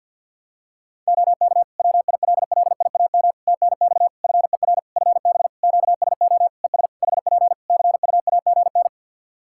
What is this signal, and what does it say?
Morse code, heard as a keyed tone, about 52 wpm, 700 Hz. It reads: OK WILLIAM TNX FER FB QSO ES HP CUAGN
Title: Practice QSO - Exchange 4 - Ending - 50wpm